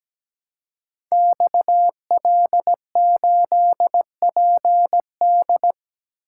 Morse code, heard as a keyed tone, about 17 wpm, 700 Hz. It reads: XL8PD
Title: Code Groups - 5 Letter:Numbers - 17wpm